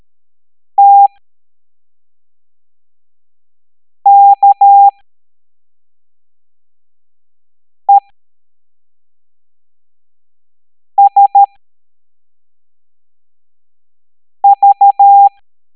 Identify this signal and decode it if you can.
Morse code, heard as a keyed tone, about 13 wpm, spaced as for 3 wpm, 800 Hz. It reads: TKESV
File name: hamdata / L05-EX01-SVY